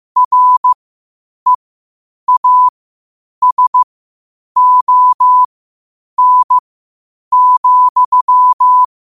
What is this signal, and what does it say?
Morse code, heard as a keyed tone, about 15 wpm, spaced as for 8 wpm, 1000 Hz. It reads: REASON,
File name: qotdfiles / 8wpm